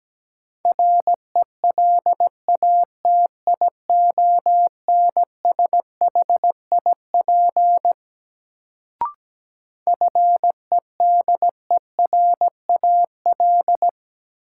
Morse code, heard as a keyed tone, about 17 wpm, 700 Hz. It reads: RELATIONSHIP FEDERAL
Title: Sentences from Top 1100 Words - New Word Drill Alt - 17wpm